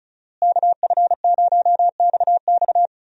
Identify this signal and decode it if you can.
Morse code, heard as a keyed tone, about 34 wpm, 700 Hz. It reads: KF0XX
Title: USA Call Signs - 5 Characters - Repeated 2x - 35wpm